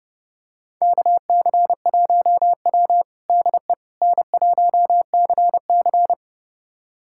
Morse code, heard as a keyed tone, about 30 wpm, 700 Hz. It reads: KC1W DE N1CC
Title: USA Callsign DE Callsign - 4 Characters - 30wpm